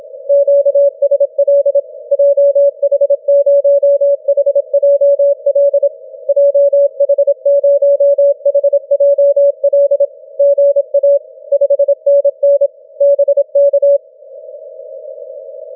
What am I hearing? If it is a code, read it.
Morse code, heard as a keyed tone, about 27 wpm, 560 Hz. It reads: QSL JH0HJL JH0HJL GA 5NN BK
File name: wp / du1ist 12m cw 31 mar 2012 0821z